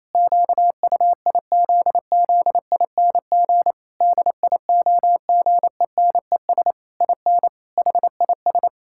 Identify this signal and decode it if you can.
Morse code, heard as a keyed tone, about 28 wpm, 700 Hz. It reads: QUIZZING DIOGENES IN HIS
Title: Pangram Challenge v2 - 28wpm